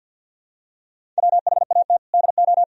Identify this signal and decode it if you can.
Morse code, heard as a keyed tone, about 46 wpm, 700 Hz. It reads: WHAT DO
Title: Sentences from Top 300 Words and Abbreviations 50wpm